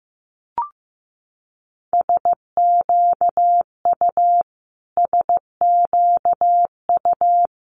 Morse code, heard as a keyed tone, about 15 wpm, 700 Hz. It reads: SQU SQU
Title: Multi-letter Phonemes Repeated 3x 15wpm